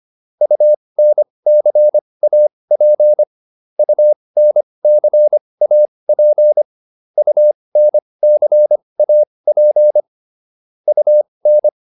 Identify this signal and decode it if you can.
Morse code, heard as a keyed tone, about 25 wpm, spaced as for 20 wpm, 600 Hz. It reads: UNCAP UNCAP UNCAP UN